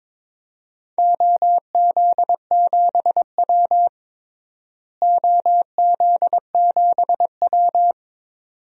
Morse code, heard as a keyed tone, about 22 wpm, 700 Hz. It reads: OZ7W OZ7W
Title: DXCC-100 Call Signs - 4 Characters - Repeated 2x - 22wpm